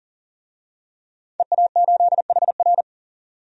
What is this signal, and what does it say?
Morse code, heard as a keyed tone, about 40 wpm, 700 Hz. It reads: EA8HR
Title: DXCC-100 Call Signs - 5 Characters - 40wpm